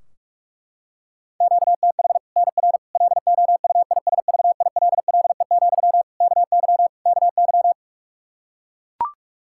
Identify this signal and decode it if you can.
Morse code, heard as a keyed tone, about 42 wpm, 700 Hz. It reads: QTH NR LOUISVILLE, KY KY
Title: K1USN SST QSO Practice 45wpm